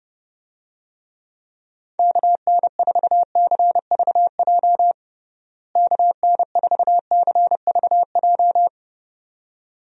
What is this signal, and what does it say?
Morse code, heard as a keyed tone, about 30 wpm, 700 Hz. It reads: KN4CVJ KN4CVJ
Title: USA Call Signs - 6 Characters - Repeated 2x - 30wpm